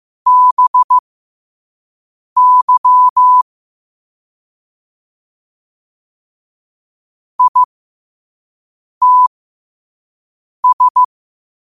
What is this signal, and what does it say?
Morse code, heard as a keyed tone, about 15 wpm, spaced as for 5 wpm, 1000 Hz. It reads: BY ITS
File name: qotdfiles / 5wpm